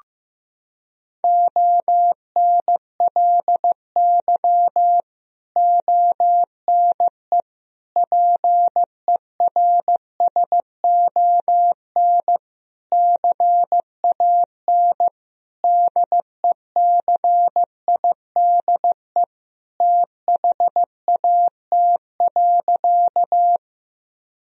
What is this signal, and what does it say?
Morse code, heard as a keyed tone, about 15 wpm, 700 Hz. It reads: ONLY ONE PERSON CAN DECIDE THAT.
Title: Sentences from Top 1000 Words - Review Part 3 - 15wpm